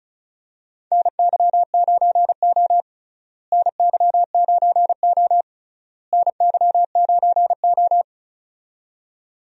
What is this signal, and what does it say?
Morse code, heard as a keyed tone, about 36 wpm, 700 Hz. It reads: NY9O NY9O NY9O
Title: USA Call Signs - 4 Characters - Repeated 3x - 35wpm